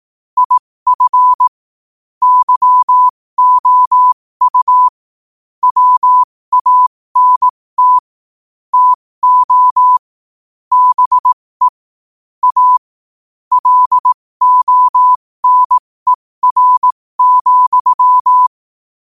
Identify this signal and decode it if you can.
Morse code, heard as a keyed tone, about 18 wpm, spaced as for 15 wpm, 1000 Hz. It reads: IF YOU WANT TO BE A LONER,
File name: qotdfiles / 15wpm